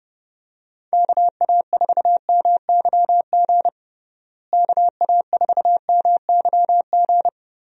Morse code, heard as a keyed tone, about 32 wpm, 700 Hz. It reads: KA4MYG KA4MYG
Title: USA Call Signs - 6 Characters - Repeated 3x - 30wpm